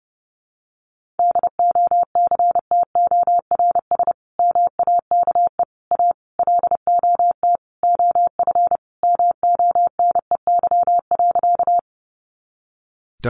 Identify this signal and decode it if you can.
Morse code, heard as a keyed tone, about 30 wpm, 700 Hz. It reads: DOCTORS MAKE A LOT OF MONEY.
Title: Easy Phrases 30wpm